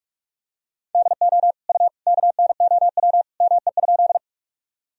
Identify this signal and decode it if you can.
Morse code, heard as a keyed tone, about 47 wpm, 700 Hz. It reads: DO U KNOW ME?
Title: Sentences from Top 200 Words and Abbreviations 45wpm